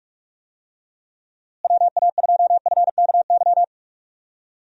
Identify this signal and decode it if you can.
Morse code, heard as a keyed tone, about 47 wpm, 700 Hz. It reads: WA2FKY
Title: USA Call Signs - 6 Characters - Repeated 3x - 45wpm